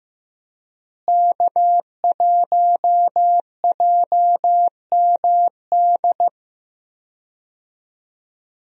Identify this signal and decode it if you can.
Morse code, heard as a keyed tone, about 15 wpm, 700 Hz. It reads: K1JMD
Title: USA Call Signs - 5 Characters - Repeated 2x - 15wpm